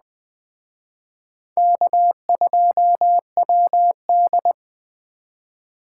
Morse code, heard as a keyed tone, about 20 wpm, 700 Hz. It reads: K2WD
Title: Speed Racing - USA 1x2 Call Signs 17wpm